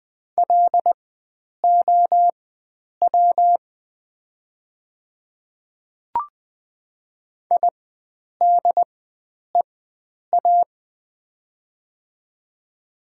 Morse code, heard as a keyed tone, about 20 wpm, spaced as for 9 wpm, 700 Hz. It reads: LOW IDEA
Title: 4x Character Spacing - Top 300 Words - 20wpm